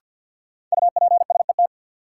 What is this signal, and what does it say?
Morse code, heard as a keyed tone, about 52 wpm, 700 Hz. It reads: UPSET
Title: Sentences from Top 800 Words - New Word Drill - 50wpm